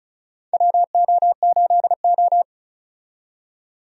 Morse code, heard as a keyed tone, about 36 wpm, 700 Hz. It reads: WO8O
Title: USA Call Signs - 4 Characters - 35wpm